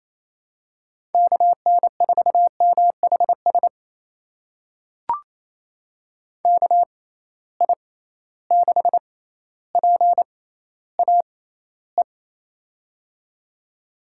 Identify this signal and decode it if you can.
Morse code, heard as a keyed tone, about 28 wpm, spaced as for 14 wpm, 700 Hz. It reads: KN4MHS KI6PAE
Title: USA Call Signs - 6 Characters - 6x Character Spacing - 28wpm